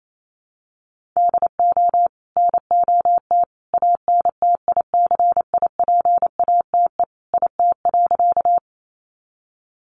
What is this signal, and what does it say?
Morse code, heard as a keyed tone, about 28 wpm, 700 Hz. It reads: DO NOT ANTICIPATE IT.
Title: Sentences from Top 1500 Words 28wpm